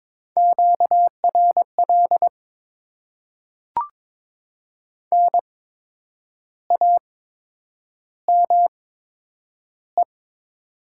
Morse code, heard as a keyed tone, about 22 wpm, spaced as for 9 wpm, 700 Hz. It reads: QRL NAME
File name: Top 20 QSO Elements - 8x Character Spacing - 22wpm